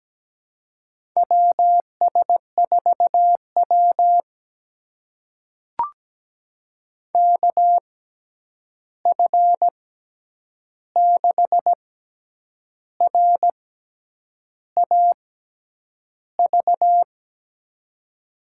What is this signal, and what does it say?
Morse code, heard as a keyed tone, about 17 wpm, spaced as for 8 wpm, 700 Hz. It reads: WS4W KF6RAV
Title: USA Call Signs - 6x Character Spacing - 17wpm